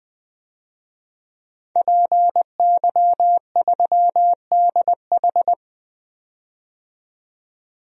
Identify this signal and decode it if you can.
Morse code, heard as a keyed tone, about 20 wpm, 700 Hz. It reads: PY3DH
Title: DXCC-100 Call Signs - 5 Characters - 20wpm